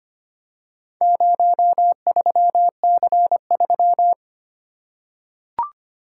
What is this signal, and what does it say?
Morse code, heard as a keyed tone, about 25 wpm, 700 Hz. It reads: 03C3
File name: practice / 25wpm / Four Letter-Number Combinations - 25wpm